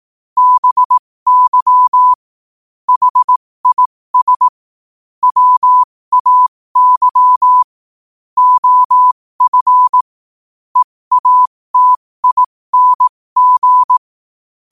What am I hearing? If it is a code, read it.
Morse code, heard as a keyed tone, about 18 wpm, spaced as for 15 wpm, 1000 Hz. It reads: BY HIS WAY OF EATING
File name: qotdfiles / 15wpm